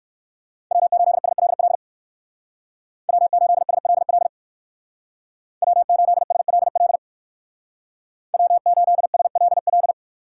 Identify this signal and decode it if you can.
Morse code, heard as a keyed tone, about 59 wpm, 700 Hz. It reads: W8SLL W8SLL W8SLL W8SLL